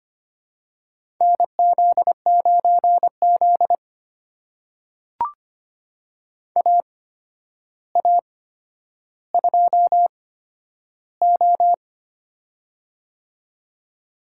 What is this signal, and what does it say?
Morse code, heard as a keyed tone, about 25 wpm, spaced as for 11 wpm, 700 Hz. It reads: NZ9Z AA2O
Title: USA Call Signs - 8x Character Spacing - 25wpm